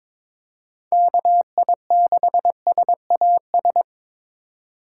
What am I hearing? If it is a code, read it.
Morse code, heard as a keyed tone, about 22 wpm, 700 Hz. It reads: KI6SAS
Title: USA Call Signs - 2x Character Spacing - 22wpm